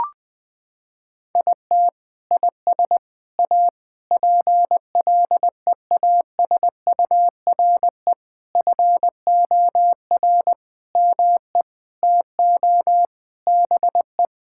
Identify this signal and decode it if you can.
Morse code, heard as a keyed tone, about 20 wpm, 700 Hz. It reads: IT IS A PLEASURE FOR ME TO BE
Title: Sentences from Top 1600 Words 20wpm